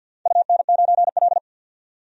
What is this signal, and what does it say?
Morse code, heard as a keyed tone, about 46 wpm, 700 Hz. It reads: UN9L
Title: DXCC-100 Call Signs - 4 Characters - 50wpm